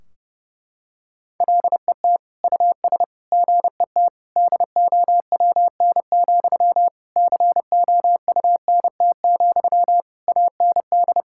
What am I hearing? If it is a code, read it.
Morse code, heard as a keyed tone, about 30 wpm, 700 Hz. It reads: LET US GET DOWN, COUNT, AND
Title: Sentences from Top 1600 Words 30wpm